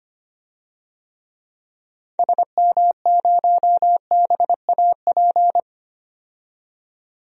Morse code, heard as a keyed tone, about 25 wpm, 700 Hz. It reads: SM0BAP
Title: DXCC-100 Call Signs - 6 Characters - 25wpm